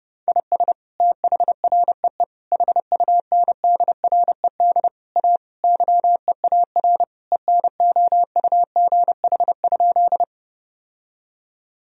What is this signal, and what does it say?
Morse code, heard as a keyed tone, about 30 wpm, 700 Hz. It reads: IS THREE HUNDRED A YEAR ENOUGH?
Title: Sentences from Top 1000 Words - Review Part 2 - 30wpm